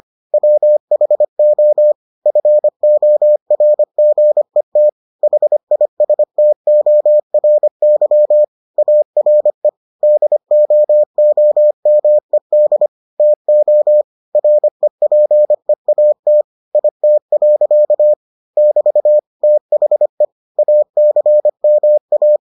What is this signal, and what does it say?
Morse code, heard as a keyed tone, about 25 wpm, 600 Hz. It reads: WHO FORGET HISTORY ARE DOOMED TO REPEAT IT. <BT> THE ACMA